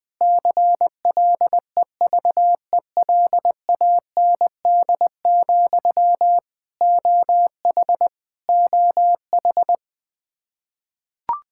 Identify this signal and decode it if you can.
Morse code, heard as a keyed tone, about 20 wpm, 700 Hz. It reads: CLEVELAND, OH OH
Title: K1USN SST QSO Practice 20wpm